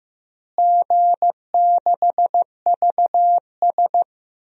Morse code, heard as a keyed tone, about 15 wpm, 700 Hz. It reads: G6VS
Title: DXCC-100 Call Signs - 4 Characters - Repeated 2x - 15wpm